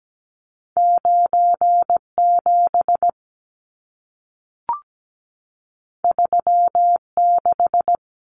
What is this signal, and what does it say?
Morse code, heard as a keyed tone, about 17 wpm, 700 Hz. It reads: 97 36
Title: Numbers - 2-Digit 17wpm